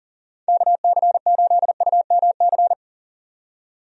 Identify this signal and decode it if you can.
Morse code, heard as a keyed tone, about 40 wpm, 700 Hz. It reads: KC8UMC
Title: USA Call Signs - 6 Characters - 40wpm